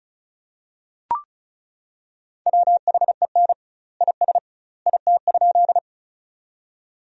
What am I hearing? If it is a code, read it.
Morse code, heard as a keyed tone, about 36 wpm, 700 Hz. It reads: WHEN IS IT?
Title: 2x Word-Spacing - Sentences from Top 100 Words - 35wpm